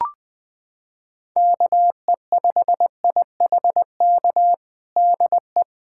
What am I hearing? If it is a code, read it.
Morse code, heard as a keyed tone, about 21 wpm, 700 Hz. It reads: KE5IHK DE
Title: USA Callsign DE Callsign - 6 Characters - 20wpm